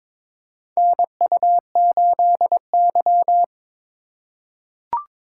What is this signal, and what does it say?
Morse code, heard as a keyed tone, about 21 wpm, 700 Hz. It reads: NU8Y
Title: USA Call Signs - 4 Characters - 4x Character Spacing - 22wpm